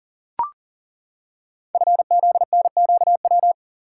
Morse code, heard as a keyed tone, about 42 wpm, 690 Hz. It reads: FZNQW
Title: Code Groups - 5 Letters - 40wpm